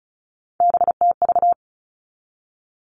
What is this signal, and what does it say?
Morse code, heard as a keyed tone, about 34 wpm, 700 Hz. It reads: BTV
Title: All Cut Numbers - 3-Digit 35wpm